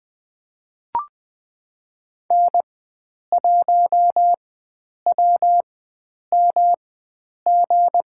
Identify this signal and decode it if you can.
Morse code, heard as a keyed tone, about 20 wpm, spaced as for 9 wpm, 700 Hz. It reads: N1WMG
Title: USA Call Signs - 4x Character Spacing - 20wpm